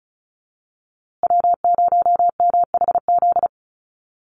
Morse code, heard as a keyed tone, about 34 wpm, 700 Hz. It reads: W0MHZ
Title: USA Call Signs Repeated 3x 35wpm